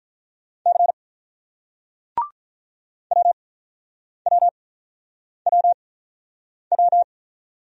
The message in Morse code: C WWWW

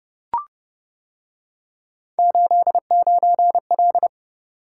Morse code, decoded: 89L

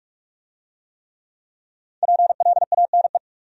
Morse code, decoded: PLANE

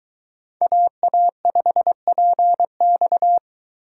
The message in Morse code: AA5PX

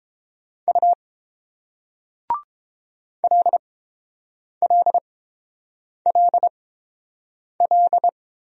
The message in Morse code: U LLLL

